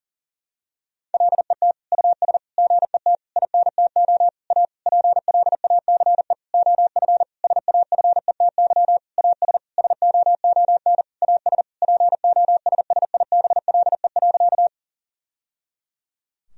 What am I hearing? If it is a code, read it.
Morse code, heard as a keyed tone, about 38 wpm, 700 Hz. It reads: LET US GET INTO A PLACE OF SAFETY AS SOON AS POSSIBLE.
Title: Sentences from Top 1000 Words - Review Part 4 - 40wpm